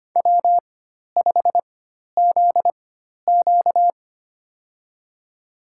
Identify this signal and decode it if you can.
Morse code, heard as a keyed tone, about 25 wpm, spaced as for 12 wpm, 700 Hz. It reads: W5ZQ
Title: USA Call Signs - 4 Characters - 4x Character Spacing - 25wpm